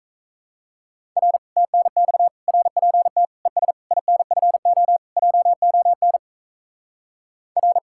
Morse code, heard as a keyed tone, about 36 wpm, 700 Hz. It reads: R TNX RPT ES INFO JON R